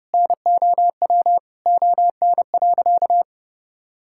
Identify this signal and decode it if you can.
Morse code, heard as a keyed tone, about 32 wpm, 700 Hz. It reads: NOW ON.
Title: Sentences from Top 1000 Words - Review Part 6 - 30wpm